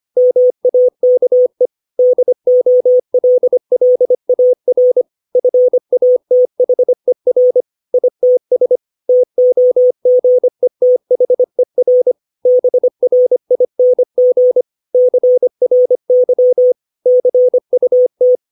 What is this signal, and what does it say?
Morse code, heard as a keyed tone, about 25 wpm, 500 Hz. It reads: MAKE DOLLAR FATHER ITS TOGETHER BRING CRY CUT
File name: 632 words / 632 Words 25WPM